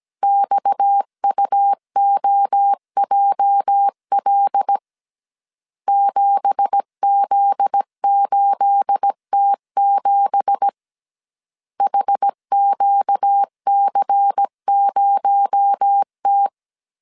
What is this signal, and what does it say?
Morse code, heard as a keyed tone, about 17 wpm, 800 Hz. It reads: XUOJL 7Z8T7 HQC0T
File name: CW / DET / 1 9 1500C S A 16wpm 2828